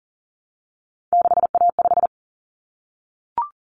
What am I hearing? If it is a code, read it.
Morse code, heard as a keyed tone, about 40 wpm, 700 Hz. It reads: 6A5
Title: Common Cut Numbers - 3-Digit 40wpm